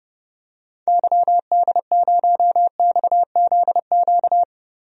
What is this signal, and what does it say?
Morse code, heard as a keyed tone, about 30 wpm, 700 Hz. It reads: YD0XZQ
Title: DXCC-100 Call Signs - 6 Characters - 30wpm